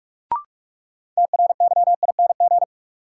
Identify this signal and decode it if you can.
Morse code, heard as a keyed tone, about 44 wpm, 700 Hz. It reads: TRYING
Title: Sets of 2 Words - New Word Drill - 45wpm